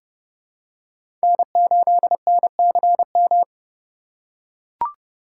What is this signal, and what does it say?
Morse code, heard as a keyed tone, about 32 wpm, 700 Hz. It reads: N8NCM
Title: USA Call Signs - 5 Characters - 4x Character Spacing - 30wpm